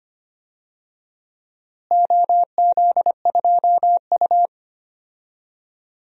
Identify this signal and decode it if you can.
Morse code, heard as a keyed tone, about 25 wpm, 700 Hz. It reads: OZ2U
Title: DXCC-100 Call Signs - 4 Characters - Repeated 2x - 25wpm